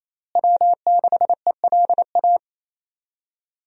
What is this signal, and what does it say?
Morse code, heard as a keyed tone, about 28 wpm, 700 Hz. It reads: W6ELA
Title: USA Call Signs - 5 Characters - Repeated 3x - 28wpm